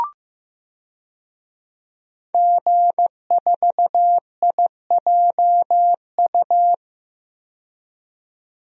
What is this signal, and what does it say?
Morse code, heard as a keyed tone, about 15 wpm, 700 Hz. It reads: G4IJU
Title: DXCC-100 Call Signs - 5 Characters - 15wpm